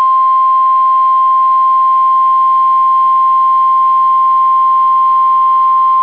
Emergency Alert System (SAME) tones (the 1050 Hz attention tone). It is heard from 0.8 to 5.7 s.